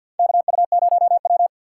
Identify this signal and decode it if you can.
Morse code, heard as a keyed tone, about 47 wpm, 700 Hz. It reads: KU0W
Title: USA Call Signs - 4 Characters - 6x Character Spacing - 50wpm